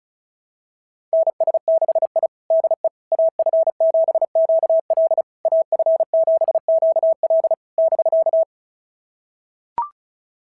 Morse code, heard as a keyed tone, about 36 wpm, 690 Hz. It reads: NS6I DE AF7QL AF7QL <BK>
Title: SOTA QSO Practice 35wpm